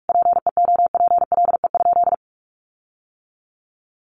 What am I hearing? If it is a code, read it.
Morse code, heard as a keyed tone, about 44 wpm, 700 Hz. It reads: PEOPLE?
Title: Sentences from Top 1500 Words 45wpm